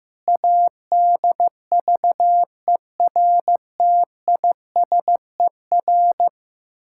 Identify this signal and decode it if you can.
Morse code, heard as a keyed tone, about 15 wpm, 700 Hz. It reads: ADVERTISER
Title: Pangram Challenge v2 - 15wpm